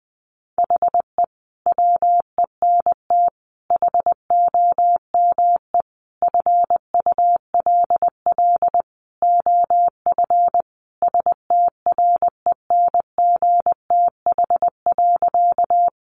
Morse code, heard as a keyed tone, about 19 wpm, 700 Hz. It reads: HE WENT HOME FULL OF STRENGTH.
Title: Sentences from Top 1300 Words 20wpm